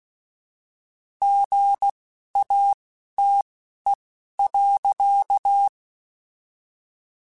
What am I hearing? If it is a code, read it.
Morse code, heard as a keyed tone, about 16 wpm, spaced as for 11 wpm, 800 Hz. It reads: GATE.
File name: Technical Subjects / Technical Subjects-mp3s / OR Gate @10